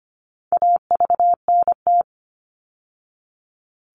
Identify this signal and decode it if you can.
Morse code, heard as a keyed tone, about 25 wpm, 700 Hz. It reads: AVNT
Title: All Cut Numbers - 4-Digit 25wpm